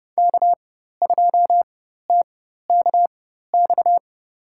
Morse code, heard as a keyed tone, about 30 wpm, spaced as for 14 wpm, 700 Hz. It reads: K2TKX
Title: USA Call Signs - 4x Character Spacing - 30wpm